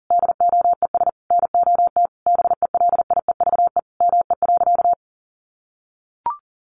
Morse code, heard as a keyed tone, about 40 wpm, 700 Hz. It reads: DOES NOT BELIEVE ME.
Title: Sentences from Top 1000 Words 40wpm